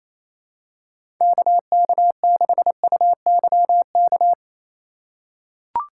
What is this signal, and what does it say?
Morse code, heard as a keyed tone, about 28 wpm, 700 Hz. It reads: KK6UYK
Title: USA Call Signs - 2x Character Spacing - 28wpm